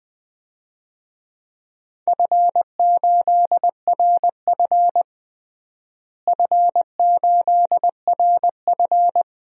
Morse code, heard as a keyed tone, about 21 wpm, 700 Hz. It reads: F8RF F8RF
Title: DXCC-100 Call Signs - 4 Characters - Repeated 2x - 20wpm